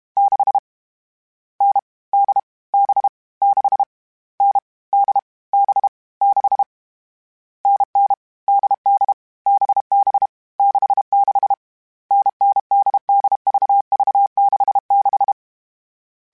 Morse code, heard as a keyed tone, about 32 wpm, 800 Hz. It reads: 6 N D B 6 N D B 6 NN DD BB 66 NNDDVV66
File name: Short Runs / Short Runs-mp3s / NDB NDB6 @30